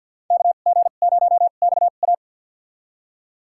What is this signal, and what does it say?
Morse code, heard as a keyed tone, about 51 wpm, spaced as for 36 wpm, 700 Hz. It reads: KG0XA
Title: USA Call Signs - 5 Characters - 2x Character Spacing - 50wpm